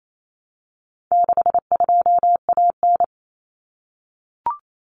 Morse code, heard as a keyed tone, about 28 wpm, 700 Hz. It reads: 62AN
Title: Common Cut Numbers - 4-Digit 28wpm